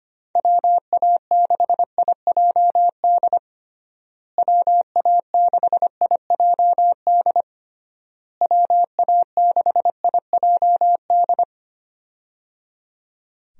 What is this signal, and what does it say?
Morse code, heard as a keyed tone, about 25 wpm, 700 Hz. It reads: WA6IJD WA6IJD WA6IJD
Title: USA Call Signs - 6 Characters - Repeated 3x - 25wpm